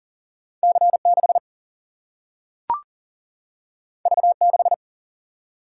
Morse code, heard as a keyed tone, about 40 wpm, 700 Hz. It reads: C 6 V 6
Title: Single-Letter Single-Number 40wpm